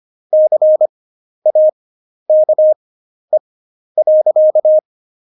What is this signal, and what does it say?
Morse code, heard as a keyed tone, about 25 wpm, spaced as for 11 wpm, 620 Hz. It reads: CAKE.
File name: Jokes-25-char-10-eff